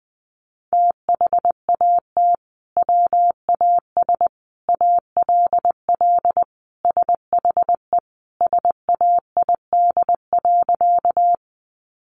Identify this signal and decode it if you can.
Morse code, heard as a keyed tone, about 20 wpm, 700 Hz. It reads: THAT WAS ALL SHE SAID.
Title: Sentences from Top 100 Words 20wpm